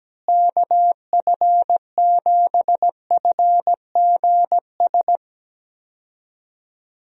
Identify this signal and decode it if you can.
Morse code, heard as a keyed tone, about 17 wpm, 700 Hz. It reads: KF7FGS
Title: USA Call Signs - 6 Characters - Repeated 3x - 17wpm